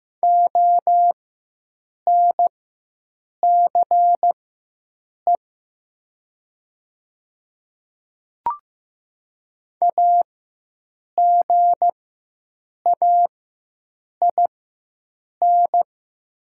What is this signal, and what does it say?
Morse code, heard as a keyed tone, about 15 wpm, spaced as for 7 wpm, 700 Hz. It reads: ONCE AGAIN
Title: 4x Character Spacing - Top 300 Words - 15wpm